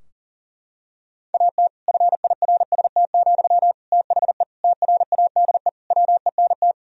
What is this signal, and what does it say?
Morse code, heard as a keyed tone, about 40 wpm, 700 Hz. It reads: AT FIRST, THE TRADE WENT
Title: Sentences from Top 1000 Words - Review Part 4 - 40wpm